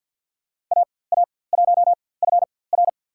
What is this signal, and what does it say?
Morse code, heard as a keyed tone, about 47 wpm, spaced as for 23 wpm, 700 Hz. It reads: AA1FR
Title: USA Call Signs - 4x Character Spacing - 50wpm